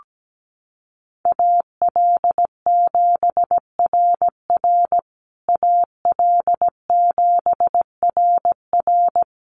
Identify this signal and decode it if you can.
Morse code, heard as a keyed tone, about 17 wpm, 700 Hz. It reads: AL7RR AL7RR
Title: USA Call Signs Repeated 3x 17wpm